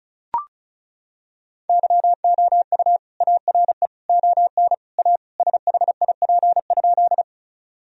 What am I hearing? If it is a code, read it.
Morse code, heard as a keyed tone, about 36 wpm, 700 Hz. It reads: YOU ARE ON A SHIP?
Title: Sentences from Top 1000 Words - Review Part 2 - 35wpm